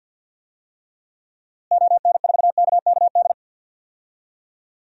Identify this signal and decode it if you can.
Morse code, heard as a keyed tone, about 53 wpm, 700 Hz. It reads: ON4KKD
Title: DXCC-100 Call Signs - 6 Characters - Repeated 2x - 50wpm